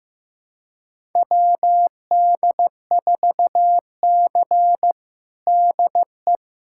Morse code, heard as a keyed tone, about 15 wpm, 700 Hz. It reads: WD4C DE
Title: USA Callsign DE Callsign - 4 Characters - 15wpm